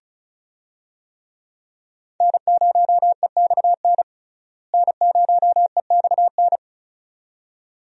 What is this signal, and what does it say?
Morse code, heard as a keyed tone, about 36 wpm, 700 Hz. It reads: N0EXN N0EXN
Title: USA Call Signs - 5 Characters - Repeated 2x - 35wpm